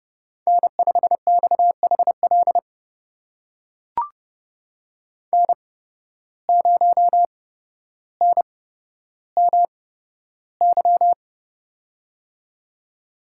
Morse code, heard as a keyed tone, about 30 wpm, spaced as for 13 wpm, 700 Hz. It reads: N5XHL N0NMY
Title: USA Call Signs - 5 Characters - 8x Character Spacing - 30wpm